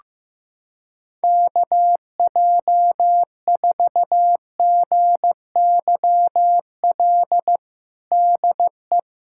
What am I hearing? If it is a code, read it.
Morse code, heard as a keyed tone, about 16 wpm, 700 Hz. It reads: KJ4GYL DE
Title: USA Callsign DE Callsign - 6 Characters - 15wpm